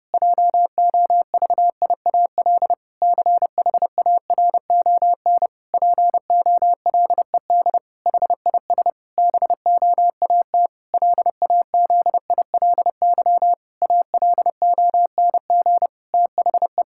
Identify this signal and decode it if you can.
Morse code, heard as a keyed tone, about 30 wpm, 700 Hz. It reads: JOVIAL CHARON POLED HIS BOAT LAZILY ALONG THE
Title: Pangram Challenge v2 - 30wpm